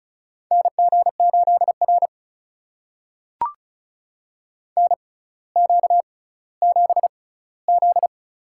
Morse code, heard as a keyed tone, about 36 wpm, spaced as for 18 wpm, 700 Hz. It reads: NG8R NQ7Z
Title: USA Call Signs - 4 Characters - 6x Character Spacing - 35wpm